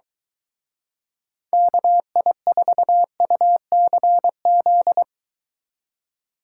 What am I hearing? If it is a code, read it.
Morse code, heard as a keyed tone, about 23 wpm, 700 Hz. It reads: KI4UCZ